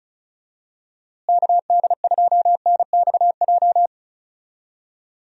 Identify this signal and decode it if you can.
Morse code, heard as a keyed tone, about 36 wpm, 700 Hz. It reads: KD2NXJ
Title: USA Call Signs - 6 Characters - Repeated 2x - 35wpm